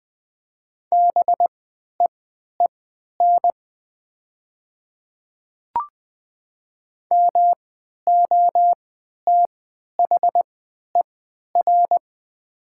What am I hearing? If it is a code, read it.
Morse code, heard as a keyed tone, about 20 wpm, spaced as for 11 wpm, 700 Hz. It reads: BEEN MOTHER
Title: 3x Character Spacing - Top 200 Words - 20wpm